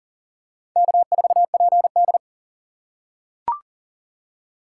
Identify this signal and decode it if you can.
Morse code, heard as a keyed tone, about 40 wpm, 700 Hz. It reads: K4PD
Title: USA Call Signs - 4 Characters - 4x Character Spacing - 40wpm